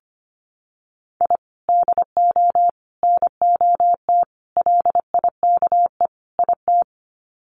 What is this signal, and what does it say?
Morse code, heard as a keyed tone, about 25 wpm, 700 Hz. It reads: I DO NOT LIKE IT
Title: Sets of 5 Words Encore 25wpm